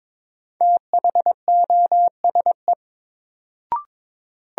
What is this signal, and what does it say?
Morse code, heard as a keyed tone, about 22 wpm, 700 Hz. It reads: THOSE
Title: Sentences from Top 300 Words - New Word Drill - 22wpm